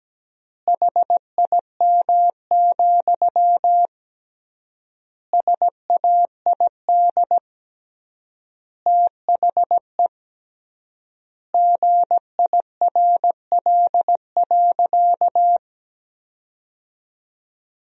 Morse code, heard as a keyed tone, about 17 wpm, 700 Hz. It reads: HIM, SAID THE GIRL.
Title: 3x Word-Spacing - Sentences from Top 400 Words - 17wpm